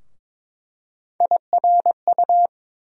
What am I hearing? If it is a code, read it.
Morse code, heard as a keyed tone, about 21 wpm, 700 Hz. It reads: IRU
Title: USA Call Sign Postfixes - 3 Letters 22wpm